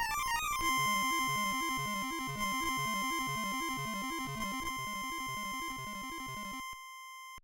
There is no code.